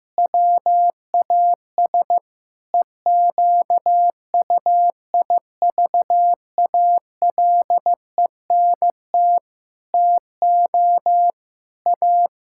Morse code, heard as a keyed tone, about 16 wpm, 700 Hz. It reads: WAS EQUIVALENT TO A
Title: Sentences from Top 1600 Words 15wpm